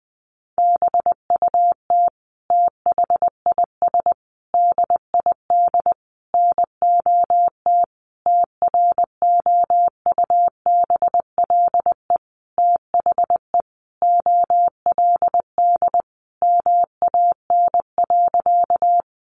Morse code, heard as a keyed tone, about 20 wpm, 700 Hz. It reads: BUT THIS DID NOT TROUBLE THE OLD MAN.
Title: Sentences from Top 1200 Words 20wpm